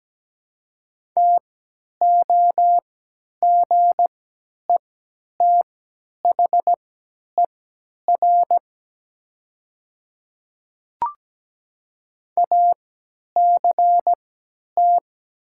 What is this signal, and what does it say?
Morse code, heard as a keyed tone, about 17 wpm, spaced as for 10 wpm, 700 Hz. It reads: TOGETHER ACT
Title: 3x Character Spacing - Top 500 Words - 17wpm